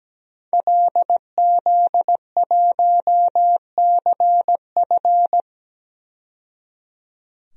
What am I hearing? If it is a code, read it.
Morse code, heard as a keyed tone, about 17 wpm, 700 Hz. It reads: LZ1CF
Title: DXCC-100 Call Signs - 5 Characters - 17wpm